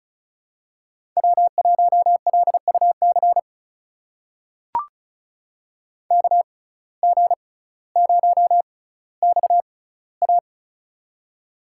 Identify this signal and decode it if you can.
Morse code, heard as a keyed tone, about 34 wpm, spaced as for 18 wpm, 700 Hz. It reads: W1LUC KG0XA
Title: USA Call Signs - 5 Characters - 6x Character Spacing - 35wpm